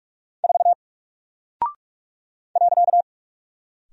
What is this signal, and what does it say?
Morse code, heard as a keyed tone, about 44 wpm, 700 Hz. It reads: 4 .
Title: Single Character (Letter:Number:Punctuation:Prosign) Alt 45wpm